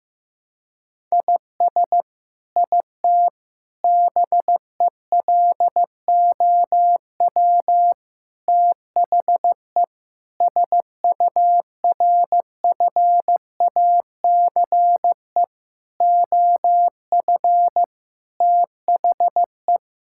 Morse code, heard as a keyed tone, about 16 wpm, 700 Hz. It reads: IS IT BELOW THE SURFACE OF THE
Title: Sentences from Top 1000 Words - Review Part 5 - 15wpm